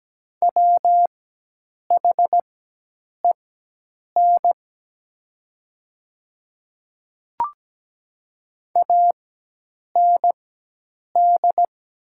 Morse code, heard as a keyed tone, about 17 wpm, spaced as for 8 wpm, 700 Hz. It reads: WHEN AND
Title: 4x Character Spacing - Top 100 Words - 17wpm